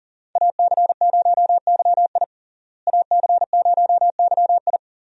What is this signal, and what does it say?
Morse code, heard as a keyed tone, about 40 wpm, 700 Hz. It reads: AC0YI AC0YI
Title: USA Call Signs - 5 Characters - Repeated 2x - 40wpm